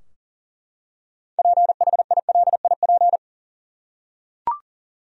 Morse code, heard as a keyed tone, about 40 wpm, 700 Hz. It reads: PHILIP
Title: USA Common Names - 40wpm